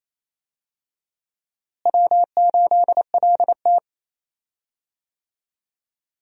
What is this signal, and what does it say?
Morse code, heard as a keyed tone, about 28 wpm, 700 Hz. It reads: W8LT
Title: USA Call Signs - 4 Characters - Repeated 3x - 28wpm